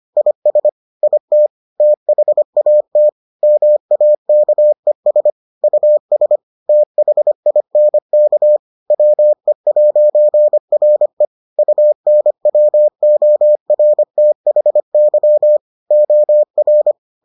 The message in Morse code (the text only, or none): IS IT THAT MAKES US THINK WE'RE UNWORTHY OR